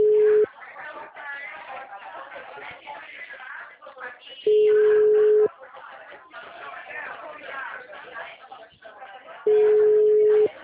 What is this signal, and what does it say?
Ringback tone (Europe, 425 Hz), with speech.